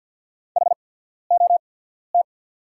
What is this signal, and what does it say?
Morse code, heard as a keyed tone, about 48 wpm, spaced as for 14 wpm, 700 Hz. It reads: HOT
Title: 8x Character Spacing - Top 200 Words - 50wpm